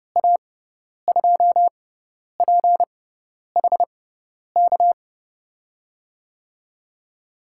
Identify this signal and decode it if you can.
Morse code, heard as a keyed tone, about 30 wpm, spaced as for 10 wpm, 700 Hz. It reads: A2PHK
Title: USA Call Signs - 6 Characters - 6x Character Spacing - 30wpm